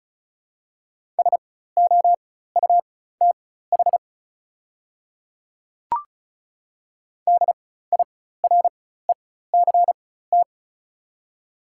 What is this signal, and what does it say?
Morse code, heard as a keyed tone, about 36 wpm, spaced as for 16 wpm, 700 Hz. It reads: SOUTH DIRECT
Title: 4x Character Spacing - Top 500 Words - 35wpm